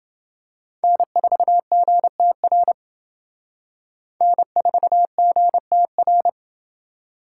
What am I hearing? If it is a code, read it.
Morse code, heard as a keyed tone, about 29 wpm, 700 Hz. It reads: N4GTR N4GTR